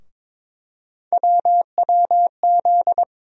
Morse code, heard as a keyed tone, about 22 wpm, 700 Hz. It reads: WWZ
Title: USA Call Sign Postfixes - 3 Letters 22wpm